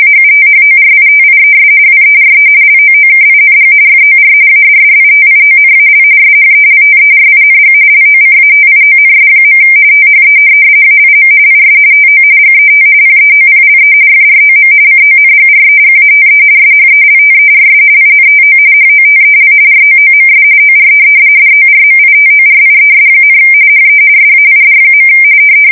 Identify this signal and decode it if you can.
RTTY, about 45.45 baud. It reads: AUTHORITIES WERE LOOKING FOR ADDITIONAL SHELTERING LOCATIONS. THE AIM IS TO GET PEOPLE FROM CONGREGATE SHELTERS INTO HOTEL ROOMS OR DORMITORIES, HE SAID.
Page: 170 Shift 2125/2295